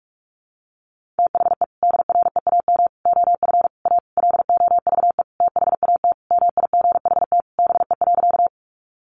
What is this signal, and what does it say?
Morse code, heard as a keyed tone, about 44 wpm, 700 Hz. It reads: THE DREAM OF A LOVE THAT MIGHT BE.
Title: Sentences from Top 1500 Words 45wpm